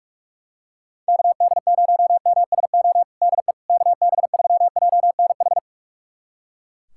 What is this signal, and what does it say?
Morse code, heard as a keyed tone, about 44 wpm, 700 Hz. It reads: KD0MSO DE KB3JNH
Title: USA Callsign DE Callsign - 6 Characters - 45wpm